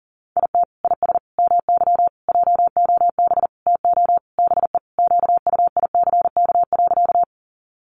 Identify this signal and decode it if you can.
Morse code, heard as a keyed tone, about 40 wpm, 700 Hz. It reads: IT IS MY JOB TO BE QUICK.
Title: Sentences from Top 600 Words 40wpm